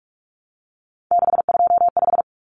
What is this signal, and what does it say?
Morse code, heard as a keyed tone, about 44 wpm, 700 Hz. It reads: 625
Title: Common Cut Numbers - 3-Digit 45wpm